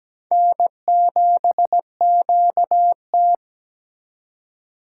Morse code, heard as a keyed tone, about 17 wpm, 700 Hz. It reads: N7QT
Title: K1USN SST QSO Practice - 17wpm